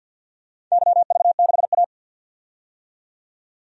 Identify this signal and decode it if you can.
Morse code, heard as a keyed tone, about 46 wpm, 700 Hz. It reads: YV6A
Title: DXCC-100 Call Signs - 4 Characters - 50wpm